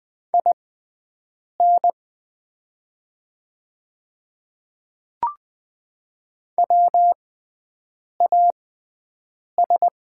Morse code, heard as a keyed tone, about 20 wpm, spaced as for 7 wpm, 700 Hz. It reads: IN WAS
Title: 6x Character Spacing - Top 100 Words - 20wpm